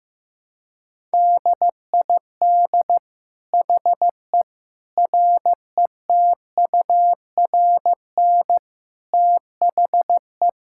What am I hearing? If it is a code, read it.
Morse code, heard as a keyed tone, about 15 wpm, 700 Hz. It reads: DID HE RETURN THE
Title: Sentences from Top 1000 Words - Review Part 5 - 15wpm